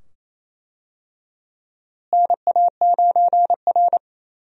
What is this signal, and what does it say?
Morse code, heard as a keyed tone, about 28 wpm, 700 Hz. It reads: NA9R
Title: USA Call Signs - 4 Characters - Repeated 2x - 28wpm